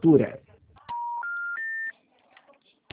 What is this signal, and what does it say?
Special information tone, with speech.